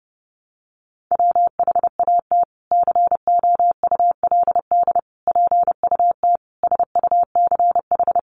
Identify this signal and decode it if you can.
Morse code, heard as a keyed tone, about 30 wpm, 700 Hz. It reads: WHAT COULD PUT SUCH
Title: Sentences from Top 600 Words 30wpm